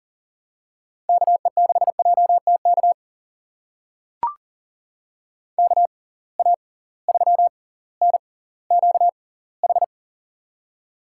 Morse code, heard as a keyed tone, about 40 wpm, spaced as for 20 wpm, 700 Hz. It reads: KE6JTK KA3NQH
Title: USA Call Signs - 6 Characters - 6x Character Spacing - 40wpm